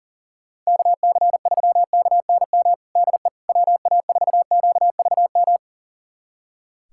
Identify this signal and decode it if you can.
Morse code, heard as a keyed tone, about 37 wpm, 700 Hz. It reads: KC3KNM DE WA4QVM
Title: USA Callsign DE Callsign - 6 Characters - 40wpm